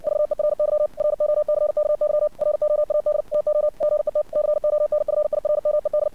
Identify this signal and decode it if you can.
Morse code, heard as a keyed tone, about 64 wpm, 600 Hz. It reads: VERY ROCKY ROAD TO GET CONFIRMED
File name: archived / CRSnet